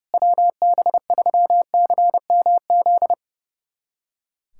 Morse code, heard as a keyed tone, about 32 wpm, 700 Hz. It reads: WB3CMZ
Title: USA Callsign DE Callsign 30wpm